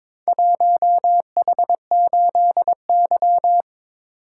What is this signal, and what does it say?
Morse code, heard as a keyed tone, about 22 wpm, 700 Hz. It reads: 1H8Y
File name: practice / 22wpm / Four Letter-Number Combinations - 22wpm